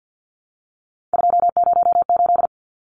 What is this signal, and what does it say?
Morse code, heard as a keyed tone, about 50 wpm, 700 Hz. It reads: 208
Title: Numbers - 3-Digit 50wpm